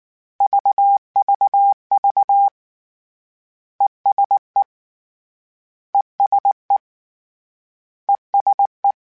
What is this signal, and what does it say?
Morse code, heard as a keyed tone, about 19 wpm, 800 Hz. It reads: VVV ESE ESE ESE
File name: General / General-mp3s / ESE ING @18